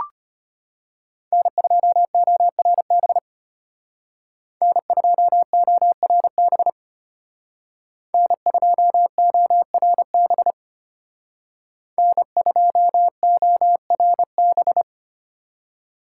Morse code, heard as a keyed tone, about 32 wpm, 700 Hz. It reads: N2ORB N2ORB N2ORB N2ORB